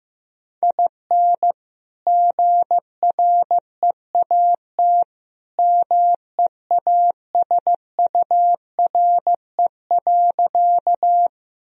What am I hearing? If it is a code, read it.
Morse code, heard as a keyed tone, about 15 wpm, 700 Hz. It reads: IN GREAT MEASURE.
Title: Sentences from Top 1000 Words - Review Part 2 - 15wpm